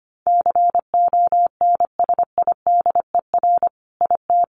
Morse code, heard as a keyed tone, about 25 wpm, 700 Hz. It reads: CONSIDER IT
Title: Sentences from Top 1400 Words 25wpm